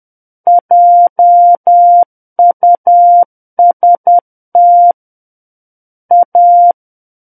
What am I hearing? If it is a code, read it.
Morse code, heard as a keyed tone, about 10 wpm, 700 Hz. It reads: JUST A
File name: Book One / 01-Chapter 1 - 10 WPM